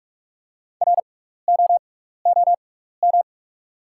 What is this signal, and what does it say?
Morse code, heard as a keyed tone, about 42 wpm, spaced as for 15 wpm, 690 Hz. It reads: ROOM
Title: 6x Character Spacing - Top 500 Words - 45wpm